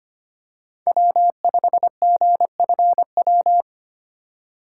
Morse code, heard as a keyed tone, about 25 wpm, 700 Hz. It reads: W5GFW
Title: USA Call Signs - 5 Characters - 4x Character Spacing - 25wpm